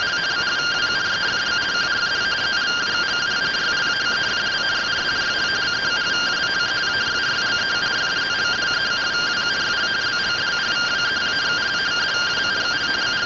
Two-tone teleprinter signal, about 170 Hz shift.